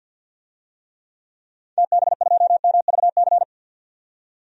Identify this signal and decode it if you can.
Morse code, heard as a keyed tone, about 51 wpm, 700 Hz. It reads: TB2MVC
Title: DXCC-100 Call Signs - 6 Characters - 50wpm